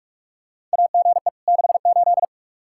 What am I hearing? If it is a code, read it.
Morse code, heard as a keyed tone, about 45 wpm, 700 Hz. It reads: AGE 68
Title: Practice QSO - Exchange 3 - More Info - 45wpm